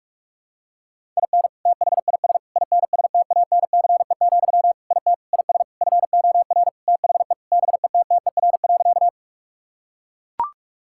Morse code, heard as a keyed tone, about 42 wpm, 690 Hz. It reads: IN THIS INSTANCE, IT IS FOR THE BETTER.
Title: Sentences from Top 1000 Words - Review Part 5 - 45wpm